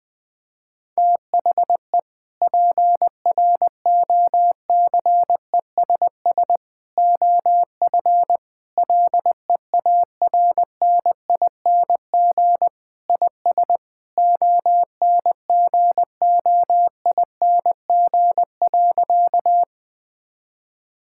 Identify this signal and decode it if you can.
Morse code, heard as a keyed tone, about 20 wpm, 700 Hz. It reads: THE PROCESS OF LEARNING IS ONGOING.
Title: Sentences with 3-Word Phrases - Spoken Beginning - 20wpm